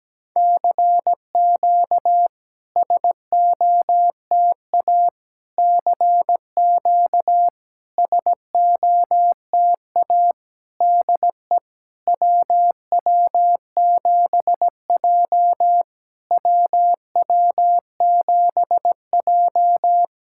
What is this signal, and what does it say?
Morse code, heard as a keyed tone, about 17 wpm, 700 Hz. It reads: CQ SOTA CQ SOTA DE WW7J WW7J
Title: SOTA QSO Practice 17wpm